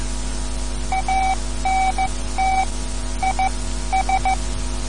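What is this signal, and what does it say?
Morse code, heard as a keyed tone, about 14 wpm, spaced as for 12 wpm, 760 Hz. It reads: ANT IS
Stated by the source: Morse code lesson, third lesson in a series for learning the Morse alphabet.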